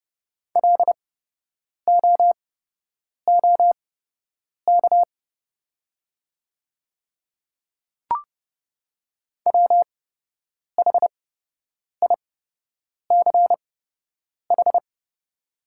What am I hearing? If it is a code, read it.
Morse code, heard as a keyed tone, about 30 wpm, spaced as for 8 wpm, 700 Hz. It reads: LOOK WHICH
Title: 8x Character Spacing - Top 100 Words - 30wpm